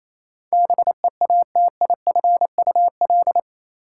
Morse code, heard as a keyed tone, about 27 wpm, 700 Hz. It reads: BEATIFUL
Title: Sentences from Top 1300 Words - New Word Drill - 28wpm